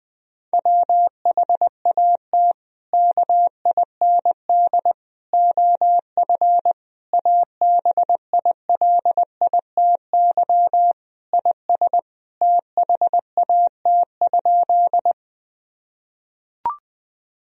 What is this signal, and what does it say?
Morse code, heard as a keyed tone, about 20 wpm, 700 Hz. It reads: WHAT KIND OF ABILITY IS THAT?
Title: Sentences from Top 1000 Words - Review Part 3 - 20wpm